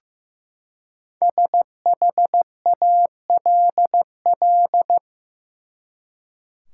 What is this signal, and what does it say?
Morse code, heard as a keyed tone, about 15 wpm, 700 Hz. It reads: SHALL
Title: Five Letter Words - 15wpm